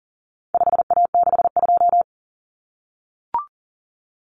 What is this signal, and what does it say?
Morse code, heard as a keyed tone, about 40 wpm, 700 Hz. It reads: 5A62
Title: Common Cut Numbers - 4-Digit 40wpm